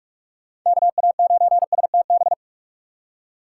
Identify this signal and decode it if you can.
Morse code, heard as a keyed tone, about 46 wpm, 690 Hz. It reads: KA9STB
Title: USA Call Signs - 4x Character Spacing - 45wpm